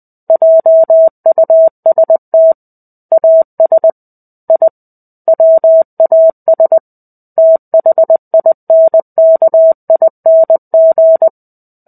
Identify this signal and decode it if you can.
Morse code, heard as a keyed tone, about 20 wpm, 650 Hz. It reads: JUST AS I WAS THINKING
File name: 05-Chapter-05 20 WPM